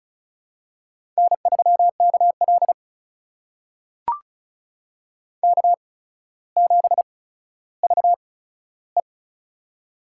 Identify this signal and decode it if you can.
Morse code, heard as a keyed tone, about 36 wpm, spaced as for 15 wpm, 700 Hz. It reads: N3KL K7VE
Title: USA Call Signs - 4 Characters - 8x Character Spacing - 35wpm